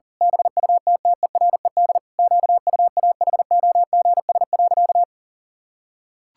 Morse code, heard as a keyed tone, about 37 wpm, 700 Hz. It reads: BUTTERED QUAHOGS.
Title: Pangram Challenge v2 - 40wpm